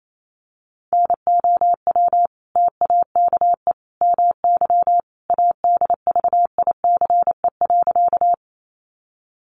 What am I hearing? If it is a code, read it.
Morse code, heard as a keyed tone, about 28 wpm, 700 Hz. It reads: NOW TAKE MY ADVICE.
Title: Sentences from Top 1100 Words 28wpm